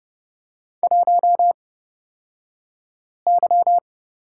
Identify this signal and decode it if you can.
Morse code, heard as a keyed tone, about 30 wpm, spaced as for 5 wpm, 700 Hz. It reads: 1Y
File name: Lesson 37d - Letter x - Two Characters 30wpm